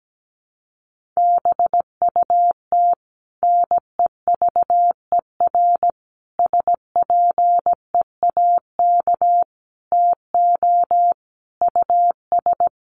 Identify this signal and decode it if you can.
Morse code, heard as a keyed tone, about 17 wpm, 700 Hz. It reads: BUT NEVER SPEAK TO US
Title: Sentences from Top 1100 Words 17wpm